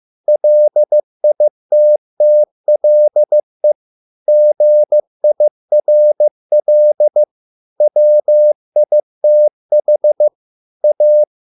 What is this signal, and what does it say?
Morse code, heard as a keyed tone, about 15 wpm, 600 Hz. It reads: LITTLE GIRL WITH A